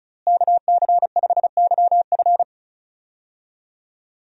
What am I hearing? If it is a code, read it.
Morse code, heard as a keyed tone, about 34 wpm, 700 Hz. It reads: KC5YF
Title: USA Call Signs - 5 Characters - Repeated 2x - 35wpm